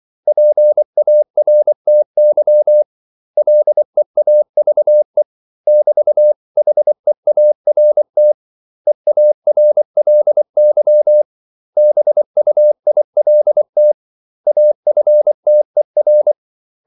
Morse code, heard as a keyed tone, about 24 wpm, 600 Hz. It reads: PARTY LEAVE <BT>HEART EARLY BUILT AFTER
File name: CW / Practice / 500MostCommonEnglishWords24